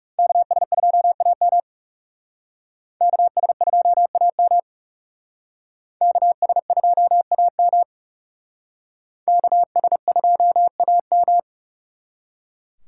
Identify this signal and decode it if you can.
Morse code, heard as a keyed tone, about 38 wpm, 700 Hz. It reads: KS2AM KS2AM KS2AM KS2AM
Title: Speed Racing - USA Call Signs 30wpm